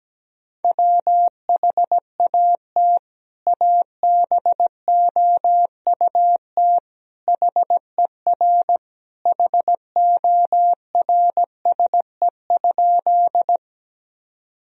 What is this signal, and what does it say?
Morse code, heard as a keyed tone, about 17 wpm, 700 Hz. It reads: WHAT ABOUT HER HORSE?
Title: Sentences from Top 1000 Words - Review Part 1 - 17wpm